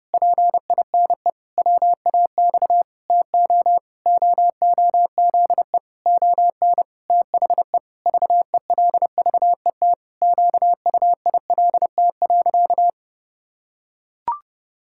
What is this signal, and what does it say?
Morse code, heard as a keyed tone, about 30 wpm, 700 Hz. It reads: PINE WAX TO OOZE ON THE VELVET QUILT.
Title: Pangram Challenge v2 - 30wpm